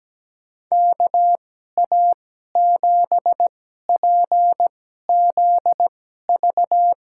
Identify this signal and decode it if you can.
Morse code, heard as a keyed tone, about 17 wpm, spaced as for 12 wpm, 700 Hz. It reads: KA7PZV
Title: USA Call Signs - 6 Characters - 2x Character Spacing - 17wpm